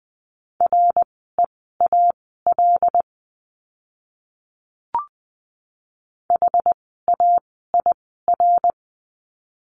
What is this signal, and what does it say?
Morse code, heard as a keyed tone, about 20 wpm, spaced as for 14 wpm, 700 Hz. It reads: REAL HAIR
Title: 101 Lesson 17f - Letter u - Words 20wpm